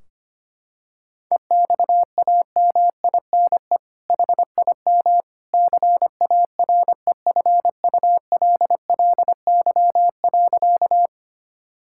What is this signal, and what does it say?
Morse code, heard as a keyed tone, about 25 wpm, 700 Hz. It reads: EXAMINE HIM CAREFULLY.
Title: Sentences from Top 1600 Words 25wpm